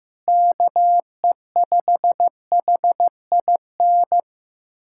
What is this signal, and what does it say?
Morse code, heard as a keyed tone, about 16 wpm, 700 Hz. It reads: KE5HIN
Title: USA Callsign DE Callsign 15wpm